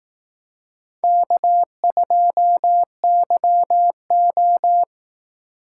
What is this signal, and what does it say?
Morse code, heard as a keyed tone, about 18 wpm, 700 Hz. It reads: K2YO
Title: Speed Racing - USA 1x2 Call Signs 15wpm